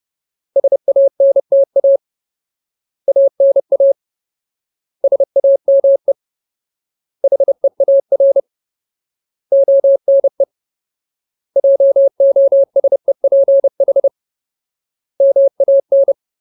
Morse code, heard as a keyed tone, about 30 wpm, 550 Hz. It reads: SANTA ANA SAME HEAR ONE JOSEPH MAN
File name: Ultimate Word List / 30WPM Ultimate